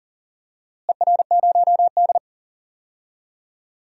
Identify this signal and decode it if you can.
Morse code, heard as a keyed tone, about 40 wpm, 700 Hz. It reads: ER0D
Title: DXCC-100 Call Signs - 4 Characters - Repeated 3x - 40wpm